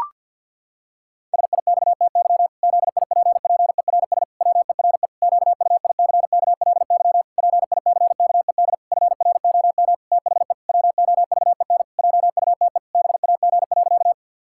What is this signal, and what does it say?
Morse code, heard as a keyed tone, about 50 wpm, 700 Hz. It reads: SIXTY ZIPPERS WERE QUICKLY PICKED FROM THE WOVEN JUTE BAG.
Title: Pangram Challenge v2 - 50wpm